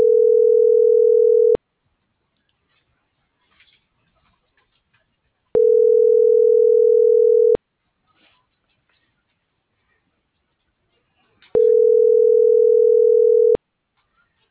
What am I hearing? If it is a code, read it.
Ringback tone (North America), 440 + 480 Hz.